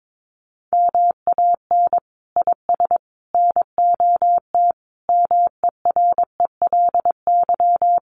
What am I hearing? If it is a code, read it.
Morse code, heard as a keyed tone, about 22 wpm, 700 Hz. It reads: MAN IS NOT MERELY
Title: Sentences from Top 1400 Words 22wpm